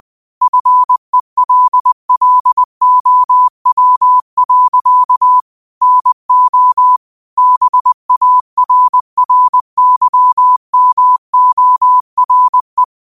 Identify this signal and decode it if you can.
Morse code, heard as a keyed tone, about 20 wpm, 1000 Hz. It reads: FELLOW. NO BARRYMORE
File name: 20wpm